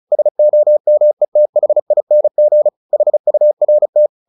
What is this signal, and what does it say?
Morse code, heard as a keyed tone, about 34 wpm, 600 Hz. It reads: SOMETHING HURT
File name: the old man and the sea / the old man and the sea 1 35wpm